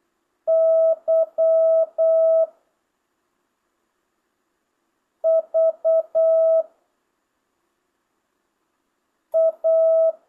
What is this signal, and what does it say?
Morse code, heard as a keyed tone, about 8 wpm, 650 Hz. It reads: Y V A
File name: jh8cbh / 2025cw